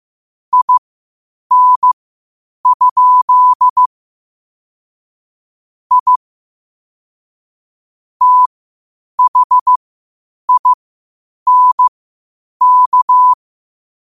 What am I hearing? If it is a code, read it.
Morse code, heard as a keyed tone, about 15 wpm, spaced as for 8 wpm, 1000 Hz. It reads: IN? I THINK
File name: qotdfiles / 8wpm